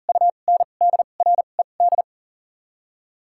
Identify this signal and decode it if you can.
Morse code, heard as a keyed tone, about 40 wpm, 700 Hz. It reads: U N D R E D
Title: 2x Character Spacing - Top 500 Words - 40wpm